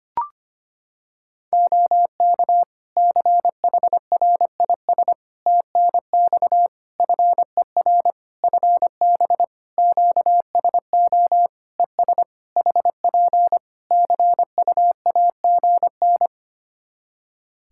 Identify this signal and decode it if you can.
Morse code, heard as a keyed tone, about 25 wpm, 700 Hz. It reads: OK CHRIS TNX FER FB QSO ES HP CUAGN
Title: Practice QSO - Exchange 4 - Ending - 25wpm